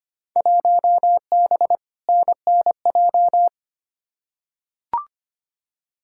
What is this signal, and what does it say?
Morse code, heard as a keyed tone, about 25 wpm, 700 Hz. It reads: 1B NNJ
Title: Field Day Exchanges Encore - 25wpm